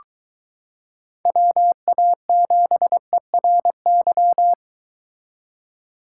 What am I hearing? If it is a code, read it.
Morse code, heard as a keyed tone, about 23 wpm, 700 Hz. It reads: WA7ERY